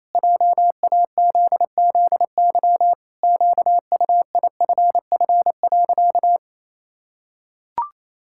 Morse code, heard as a keyed tone, about 28 wpm, 700 Hz. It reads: JAZZY QUIFF.
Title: Pangram Challenge v2 - 28wpm